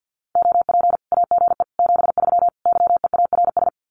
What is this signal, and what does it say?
Morse code, heard as a keyed tone, about 50 wpm, 700 Hz. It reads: OP AGE 73 YEARS
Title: Common QSO Phrases 50wpm